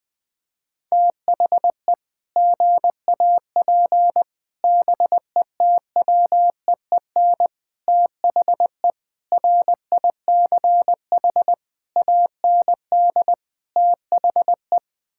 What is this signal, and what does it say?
Morse code, heard as a keyed tone, about 20 wpm, 700 Hz. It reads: THE GAP BETWEEN THE RICH AND THE
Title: Sentences with 8-Word Phrases - 20wpm